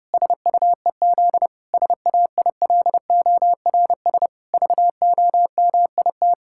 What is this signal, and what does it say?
Morse code, heard as a keyed tone, about 30 wpm, 700 Hz. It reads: SUEZ SAILORS VOMIT
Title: Pangram Challenge v2 - 30wpm